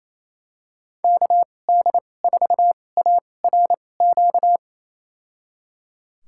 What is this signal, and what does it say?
Morse code, heard as a keyed tone, about 28 wpm, spaced as for 20 wpm, 700 Hz. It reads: KD4ARQ
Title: USA Call Signs - 6 Characters - 2x Character Spacing - 28wpm